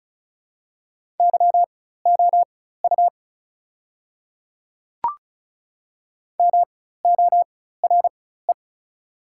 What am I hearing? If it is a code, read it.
Morse code, heard as a keyed tone, about 34 wpm, spaced as for 16 wpm, 690 Hz. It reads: YOU MORE
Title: 4x Character Spacing - Top 100 Words - 35wpm